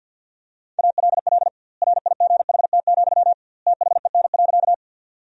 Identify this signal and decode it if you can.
Morse code, heard as a keyed tone, about 52 wpm, 700 Hz. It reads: ALL RIGHT, THEN.
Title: 2x Word-Spacing - Sentences from Top 200 Words - 50wpm